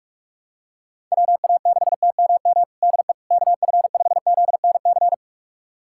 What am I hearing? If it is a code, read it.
Morse code, heard as a keyed tone, about 46 wpm, 700 Hz. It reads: WA6TMM DE KF5ZNC
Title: USA Callsign DE Callsign - 6 Characters - 45wpm